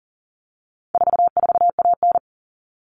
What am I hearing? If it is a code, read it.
Morse code, heard as a keyed tone, about 40 wpm, 700 Hz. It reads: 44AN
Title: All Cut Numbers - 4-Digit 40wpm